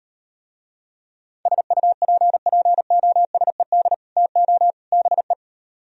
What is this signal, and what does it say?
Morse code, heard as a keyed tone, about 38 wpm, 700 Hz. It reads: SUPPOSED TO BE